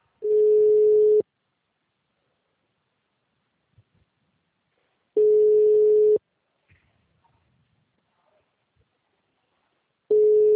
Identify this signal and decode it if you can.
Ringback tone (Europe, 425 Hz).